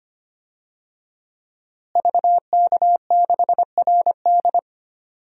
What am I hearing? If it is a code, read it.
Morse code, heard as a keyed tone, about 25 wpm, 700 Hz. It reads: VK6RD
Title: DXCC-100 Call Signs - 5 Characters - Repeated 3x - 25wpm